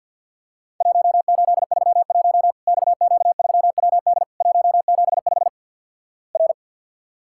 Morse code, heard as a keyed tone, about 50 wpm, 700 Hz. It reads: 1831 XQ3WD 175 R
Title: Worked All Europe DX Contest - QTC Exchange - 50wpm